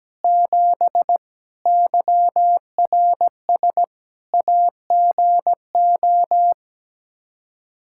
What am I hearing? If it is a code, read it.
Morse code, heard as a keyed tone, about 17 wpm, 700 Hz. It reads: 7 YRS AGO
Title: Practice QSO - Exchange 3 - More Info - 17wpm